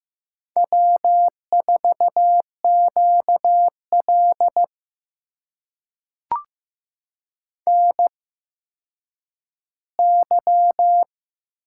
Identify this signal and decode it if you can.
Morse code, heard as a keyed tone, about 15 wpm, spaced as for 9 wpm, 700 Hz. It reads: W4QL NY
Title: USA Call Signs - 4 Characters - 8x Character Spacing - 15wpm